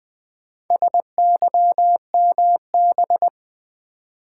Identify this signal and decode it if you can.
Morse code, heard as a keyed tone, about 20 wpm, 700 Hz. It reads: SYMB
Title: Build A Word - 2 Letters At A Time - 8-Letter Words - 20wpm